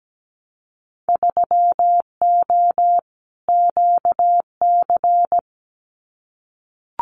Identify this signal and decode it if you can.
Morse code, heard as a keyed tone, about 17 wpm, 700 Hz. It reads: 3O QC
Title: Winter Field Day Exchanges Encore 17wpm